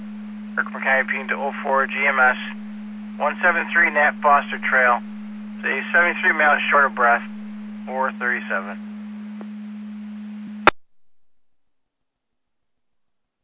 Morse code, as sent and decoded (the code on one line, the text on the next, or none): none